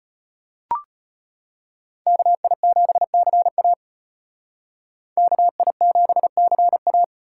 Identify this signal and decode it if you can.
Morse code, heard as a keyed tone, about 33 wpm, 700 Hz. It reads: KI7CA KI7CA